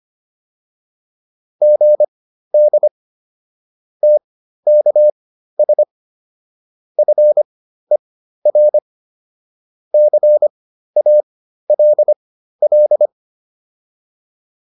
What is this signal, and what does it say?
Morse code, heard as a keyed tone, about 25 wpm, spaced as for 13 wpm, 600 Hz. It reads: GD TKS FER CALL
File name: QSO402 13